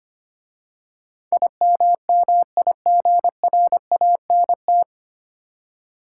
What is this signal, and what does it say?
Morse code, heard as a keyed tone, about 25 wpm, 700 Hz. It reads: IMMIGRANT
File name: practice / 25wpm / 9 Letter Words - First 3 Letters Given - 25wpm